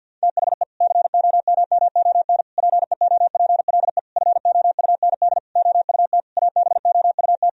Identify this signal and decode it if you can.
Morse code, heard as a keyed tone, about 52 wpm, 700 Hz. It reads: THE COMMON PEOPLE FOUND OUT ABOUT
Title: Sentences from Top 1000 Words - Review Part 3 - 50wpm